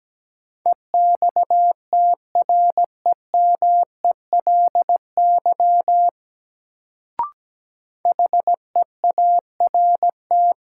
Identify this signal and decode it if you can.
Morse code, heard as a keyed tone, about 17 wpm, 700 Hz. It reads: EXTREMELY HEART
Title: Sentences from Top 1200 Words - New Word Drill Alt - 17wpm